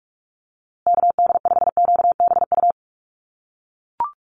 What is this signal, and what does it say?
Morse code, heard as a keyed tone, about 44 wpm, 700 Hz. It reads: KD5QBU
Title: USA Call Signs Repeated 3x 45wpm